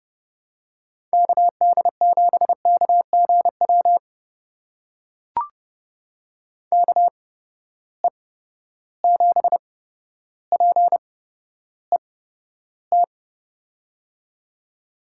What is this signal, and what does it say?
Morse code, heard as a keyed tone, about 30 wpm, spaced as for 13 wpm, 700 Hz. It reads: KD7KGW KE7PET
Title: USA Call Signs - 6 Characters - 8x Character Spacing - 30wpm